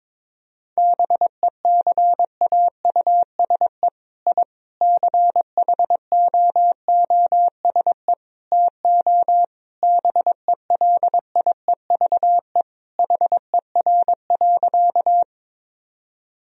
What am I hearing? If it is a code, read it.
Morse code, heard as a keyed tone, about 22 wpm, 700 Hz. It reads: BECAUSE I CHOOSE TO BELIEVE HER.
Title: Sentences from Top 1000 Words - Review Part 4 - 22wpm